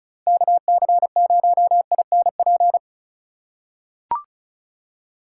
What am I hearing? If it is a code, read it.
Morse code, heard as a keyed tone, about 36 wpm, 700 Hz. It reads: KC0INP
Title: K1USN SST QSO Practice - 35wpm